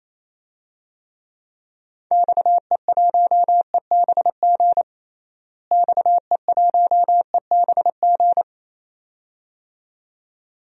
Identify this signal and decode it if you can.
Morse code, heard as a keyed tone, about 27 wpm, 700 Hz. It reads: XE1EBG XE1EBG
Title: DXCC-100 Call Signs - 6 Characters - Repeated 2x - 28wpm